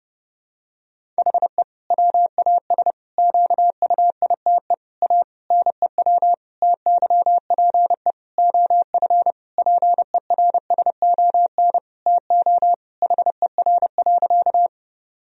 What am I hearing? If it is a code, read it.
Morse code, heard as a keyed tone, about 32 wpm, 700 Hz. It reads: HE WAS QUITE A NEW TYPE OF PERSON TO HER.
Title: Sentences from Top 1000 Words - Review Part 3 - 30wpm